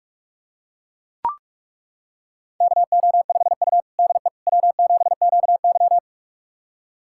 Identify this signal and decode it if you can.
Morse code, heard as a keyed tone, about 43 wpm, 700 Hz. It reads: KO5U DE W7QY
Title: USA Callsign DE Callsign - 4 Characters - 45wpm